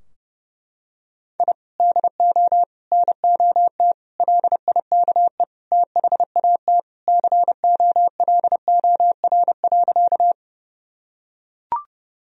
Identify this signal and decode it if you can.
Morse code, heard as a keyed tone, about 30 wpm, 700 Hz. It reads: I DO NOT LIKE THAT COLOR.
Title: Sentences from Top 1000 Words - Review Part 2 - 30wpm